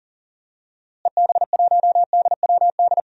Morse code, heard as a keyed tone, about 40 wpm, 700 Hz. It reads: EB1DWD
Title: DXCC-100 Call Signs - 6 Characters - Repeated 3x - 40wpm